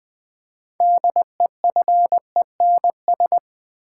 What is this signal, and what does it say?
Morse code, heard as a keyed tone, about 20 wpm, 700 Hz. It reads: DEFENS
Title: Build A Word - 3 Letters At A Time - 9-Letter Words - 20wpm